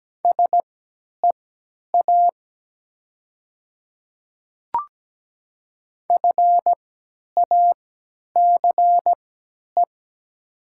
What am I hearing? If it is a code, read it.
Morse code, heard as a keyed tone, about 17 wpm, spaced as for 10 wpm, 700 Hz. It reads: SEA FACE
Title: 3x Character Spacing - Top 300 Words - 17wpm